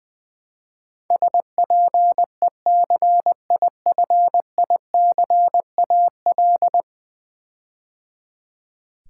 Morse code, heard as a keyed tone, about 20 wpm, 700 Hz. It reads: SPECIFICAL
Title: Build A Word - 5 Letters At A Time - 12-Letter Words - 20wpm